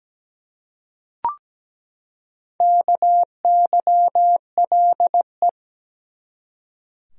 Morse code, heard as a keyed tone, about 17 wpm, 700 Hz. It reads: KYLE
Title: USA Common Names - Max 5 Letters - 17wpm